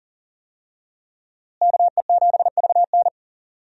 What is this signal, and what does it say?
Morse code, heard as a keyed tone, about 40 wpm, 700 Hz. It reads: KE7VN
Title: USA Call Signs - 5 Characters - Repeated 2x - 40wpm